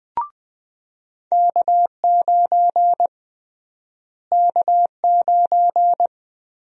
Morse code, heard as a keyed tone, about 20 wpm, 700 Hz. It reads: K9 K9
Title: USA Call Sign Prefixes - Repeated 2x - 20wpm